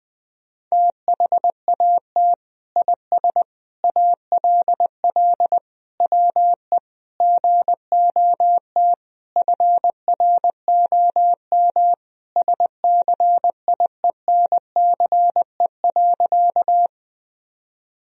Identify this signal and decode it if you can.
Morse code, heard as a keyed tone, about 20 wpm, 700 Hz. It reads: THAT IS ALL WE GOT FROM SCIENCE.
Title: Sentences from Top 1000 Words - Review Part 2 - 20wpm